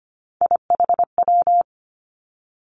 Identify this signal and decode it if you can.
Morse code, heard as a keyed tone, about 25 wpm, 700 Hz. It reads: IHW
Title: Three Challenging Letter-Number Combinations 25wpm